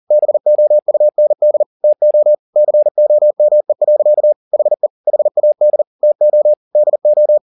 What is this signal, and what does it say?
Morse code, heard as a keyed tone, about 40 wpm, 600 Hz. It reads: BOUND TO COME. HE HAD TO DO
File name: the old man and the sea 2 40wpm